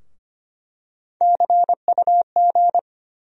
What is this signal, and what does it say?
Morse code, heard as a keyed tone, about 25 wpm, 700 Hz. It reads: CUG
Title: USA Call Sign Postfixes - 3 Letters 25wpm